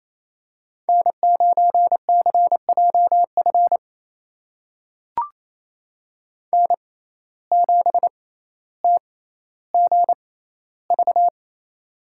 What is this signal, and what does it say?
Morse code, heard as a keyed tone, about 28 wpm, spaced as for 14 wpm, 700 Hz. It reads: N9CJF N7TGV
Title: USA Call Signs - 5 Characters - 6x Character Spacing - 28wpm